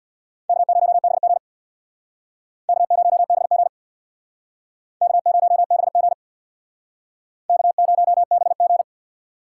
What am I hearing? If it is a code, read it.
Morse code, heard as a keyed tone, about 65 wpm, 700 Hz. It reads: K0BG K0BG K0BG K0BG